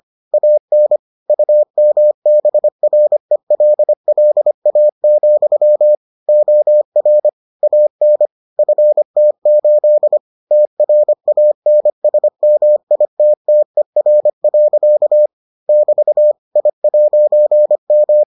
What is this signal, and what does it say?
Morse code, heard as a keyed tone, about 25 wpm, 600 Hz. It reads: AN UMBRELLA, OR AN FT8 TRANSMITTER. <BT> I'M